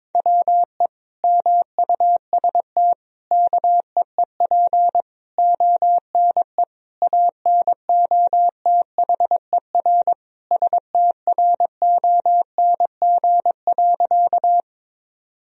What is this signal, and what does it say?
Morse code, heard as a keyed tone, about 22 wpm, 700 Hz. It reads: WE MUST KEEP ONE ANOTHER STRONG.
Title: Sentences from Top 1000 Words - Review Part 3 - 22wpm